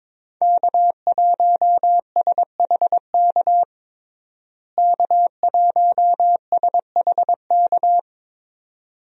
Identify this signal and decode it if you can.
Morse code, heard as a keyed tone, about 22 wpm, 700 Hz. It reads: K1SHK K1SHK
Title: USA Call Signs - 5 Characters - Repeated 2x - 22wpm